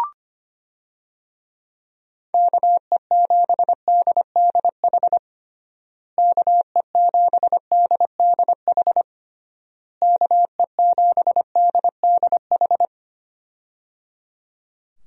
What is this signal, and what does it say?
Morse code, heard as a keyed tone, about 25 wpm, 700 Hz. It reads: KE7DDH KE7DDH KE7DDH
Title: USA Call Signs - 6 Characters - Repeated 3x - 25wpm